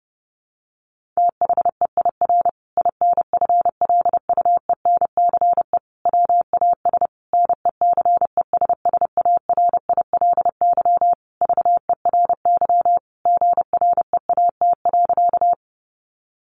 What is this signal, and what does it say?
Morse code, heard as a keyed tone, about 30 wpm, 700 Hz. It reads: THEIR INFLUENCE WAS NECESSARILY VERY GREAT.
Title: Sentences from Top 1400 Words 30wpm